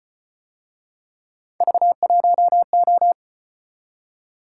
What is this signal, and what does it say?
Morse code, heard as a keyed tone, about 33 wpm, 700 Hz. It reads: V1O